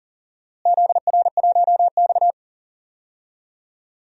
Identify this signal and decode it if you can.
Morse code, heard as a keyed tone, about 40 wpm, 700 Hz. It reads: ZR1X
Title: DXCC-100 Call Signs - 4 Characters - Repeated 3x - 40wpm